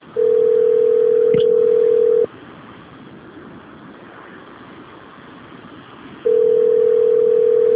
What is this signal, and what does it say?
Ringback tone (North America), 440 + 480 Hz.